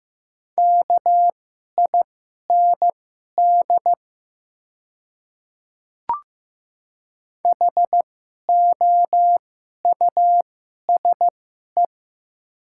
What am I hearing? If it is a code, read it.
Morse code, heard as a keyed tone, about 15 wpm, spaced as for 11 wpm, 700 Hz. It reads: KIND HOUSE
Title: 2x Character Spacing - Top 200 Words - 15wpm